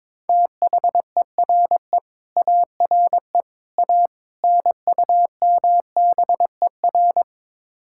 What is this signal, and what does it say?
Morse code, heard as a keyed tone, about 23 wpm, 700 Hz. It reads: THERE ARE A NUMBER
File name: practice / 22wpm / Build A Phrase - 5-Word Phrases - 22wpm